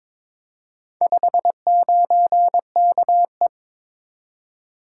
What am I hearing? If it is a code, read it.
Morse code, heard as a keyed tone, about 22 wpm, 700 Hz. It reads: 59KE
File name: practice / 22wpm / Four Letter-Number Combinations - 22wpm